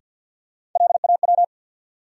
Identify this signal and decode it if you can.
Morse code, heard as a keyed tone, about 47 wpm, 700 Hz. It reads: LAW
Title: Sentences from Top 600 Words - New Word Drill - 50wpm